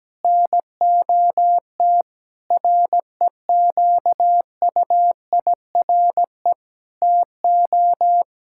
Morse code, heard as a keyed tone, about 17 wpm, 700 Hz. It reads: NOT REQUIRE TO
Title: Sentences from Top 1600 Words 17wpm